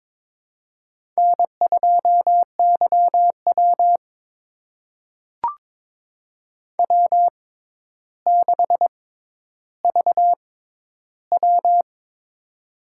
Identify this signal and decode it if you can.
Morse code, heard as a keyed tone, about 22 wpm, spaced as for 11 wpm, 700 Hz. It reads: N2YW W6VW
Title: USA Call Signs - 4 Characters - 6x Character Spacing - 22wpm